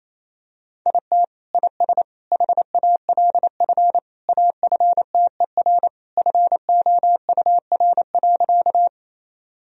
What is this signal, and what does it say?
Morse code, heard as a keyed tone, about 28 wpm, 700 Hz. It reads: IT IS HALF AFTER FOUR.
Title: Sentences from Top 1000 Words - Review Part 1 - 28wpm